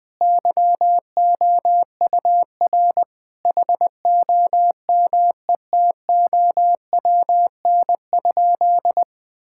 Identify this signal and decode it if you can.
Morse code, heard as a keyed tone, about 20 wpm, 700 Hz. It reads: YOUR HOMETOWN?
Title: Sentences with 8-Word Phrases - 20wpm